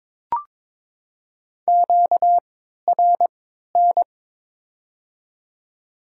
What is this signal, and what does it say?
Morse code, heard as a keyed tone, about 22 wpm, spaced as for 13 wpm, 690 Hz. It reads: QRN
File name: Top 90 QSO Elements - 3x Character Spacing - 22wpm